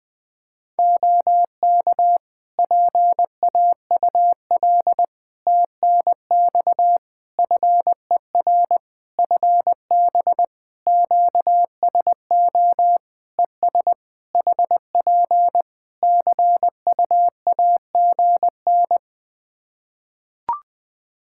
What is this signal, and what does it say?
Morse code, heard as a keyed tone, about 20 wpm, 700 Hz. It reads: OK PAUL TNX FER FB QSO ES HP CUAGN
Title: K1USN SST QSO Practice 20wpm